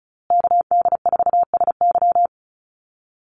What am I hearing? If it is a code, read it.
Morse code, heard as a keyed tone, about 34 wpm, 700 Hz. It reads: KD4SY
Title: USA Call Signs Repeated 3x 35wpm